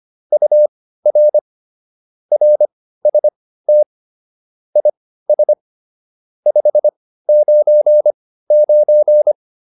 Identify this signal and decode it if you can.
Morse code, heard as a keyed tone, about 25 wpm, spaced as for 15 wpm, 600 Hz. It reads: UR RST IS 599